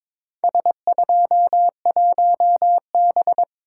Morse code, heard as a keyed tone, about 23 wpm, 700 Hz. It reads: S21B
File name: practice / 22wpm / Four Letter-Number Combinations - 22wpm